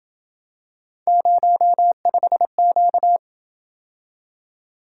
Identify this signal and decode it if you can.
Morse code, heard as a keyed tone, about 27 wpm, 700 Hz. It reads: 05Q